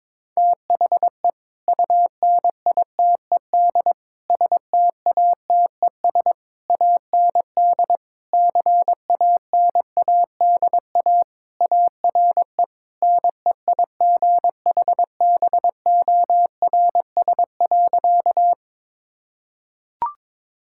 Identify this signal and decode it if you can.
Morse code, heard as a keyed tone, about 22 wpm, 700 Hz. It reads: THE UNITED STATES AND CANADA ARE NEIGHBORS.
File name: practice / 22wpm / Sentences with 3-Word Phrases - 22wpm